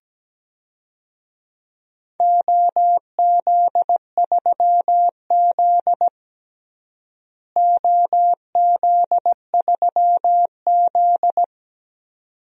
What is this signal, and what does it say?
Morse code, heard as a keyed tone, about 17 wpm, 700 Hz. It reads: OZ3Z OZ3Z
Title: DXCC-100 Call Signs - 4 Characters - Repeated 2x - 17wpm